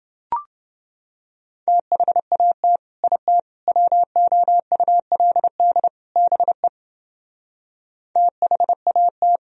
Morse code, heard as a keyed tone, about 31 wpm, 700 Hz. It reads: THAT IT WOULD BE THAT